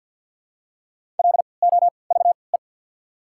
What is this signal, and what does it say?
Morse code, heard as a keyed tone, about 53 wpm, spaced as for 29 wpm, 700 Hz. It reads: LOVE
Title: 3x Character Spacing - Top 500 Words - 50wpm